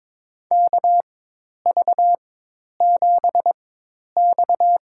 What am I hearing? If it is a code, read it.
Morse code, heard as a keyed tone, about 22 wpm, spaced as for 10 wpm, 700 Hz. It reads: KV7X
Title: USA Call Signs - 4 Characters - 4x Character Spacing - 22wpm